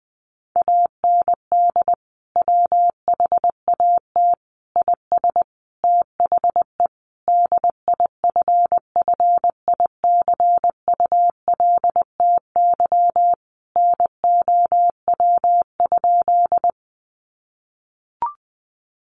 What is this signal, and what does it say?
Morse code, heard as a keyed tone, about 20 wpm, 700 Hz. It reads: AND WHAT IS THE DIFFICULTY NOW?
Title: Sentences from Top 1500 Words 20wpm